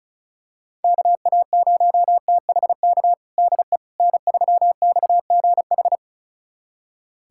Morse code, heard as a keyed tone, about 34 wpm, 700 Hz. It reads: KA0THK DE N3XGH
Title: USA Callsign DE Callsign 35wpm